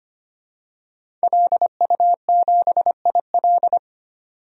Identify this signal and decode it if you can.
Morse code, heard as a keyed tone, about 25 wpm, 700 Hz. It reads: LU7IL
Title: DXCC-100 Call Signs - 5 Characters - 25wpm